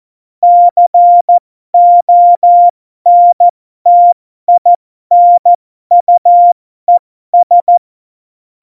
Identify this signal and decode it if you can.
Morse code, heard as a keyed tone, about 14 wpm, spaced as for 12 wpm, 700 Hz. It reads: CONTINUES
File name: Morsekurs / Witze